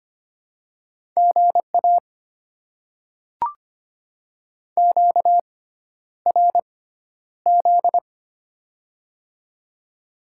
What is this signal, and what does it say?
Morse code, heard as a keyed tone, about 25 wpm, spaced as for 11 wpm, 700 Hz. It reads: GA QRZ
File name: Top 90 QSO Elements - 6x Character Spacing - 25wpm